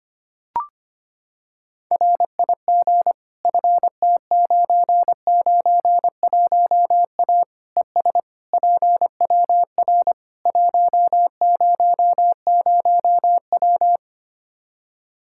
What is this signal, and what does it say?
Morse code, heard as a keyed tone, about 25 wpm, 700 Hz. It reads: RIG FT991A ES PWR 100W
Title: K1USN SST QSO Practice 25wpm